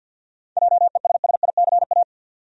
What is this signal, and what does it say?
Morse code, heard as a keyed tone, about 52 wpm, 700 Hz. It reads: JESSICA
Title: USA Common Names - 50wpm